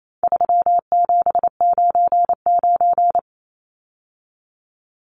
Morse code, heard as a keyed tone, about 28 wpm, 700 Hz. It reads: 3799
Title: Numbers - 4-Digit 28wpm